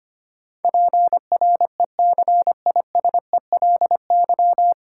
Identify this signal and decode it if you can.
Morse code, heard as a keyed tone, about 25 wpm, 700 Hz. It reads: PRECISELY
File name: Nine Letter Words - 25wpm